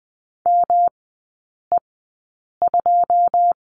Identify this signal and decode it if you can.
Morse code, heard as a keyed tone, about 20 wpm, spaced as for 8 wpm, 700 Hz. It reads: ME2
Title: 138 Lesson 23e - Letter f - Three Characters 20wpm